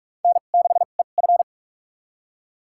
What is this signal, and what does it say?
Morse code, heard as a keyed tone, about 41 wpm, spaced as for 31 wpm, 700 Hz. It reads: N6EF
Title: USA Call Signs - 4 Characters - 2x Character Spacing - 45wpm